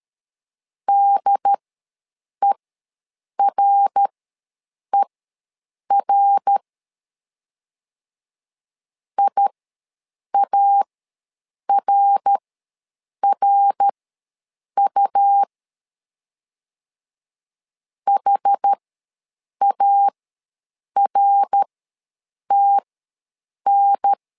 Morse code, heard as a keyed tone, about 13 wpm, spaced as for 7 wpm, 800 Hz. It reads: DERER IARRU HARTN
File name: LEZIONI / 3 MWRO / 3 MWR ws 12wpm 1809